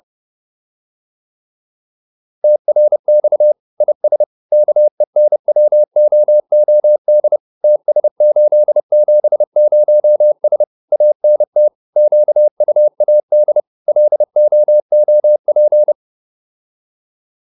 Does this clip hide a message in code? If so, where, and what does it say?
Yes: Morse code from 2.4 to 15.9 s, heard as a keyed tone, about 30 wpm, 600 Hz. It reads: TRX IS KENWOOD TS870S ANT QUAD LOOP